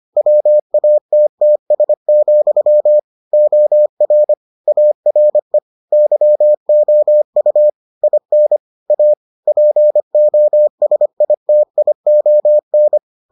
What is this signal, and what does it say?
Morse code, heard as a keyed tone, about 25 wpm, 600 Hz. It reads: WATTS, OR ARE YOU IN A POSITION